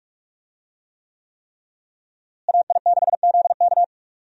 Morse code, heard as a keyed tone, about 46 wpm, 700 Hz. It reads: AI6ZK
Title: USA Call Signs - 5 Characters - Repeated 3x - 45wpm